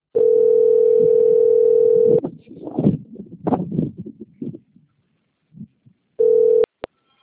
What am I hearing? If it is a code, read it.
Ringback tone (North America), 440 + 480 Hz.
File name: out- - - - - .18648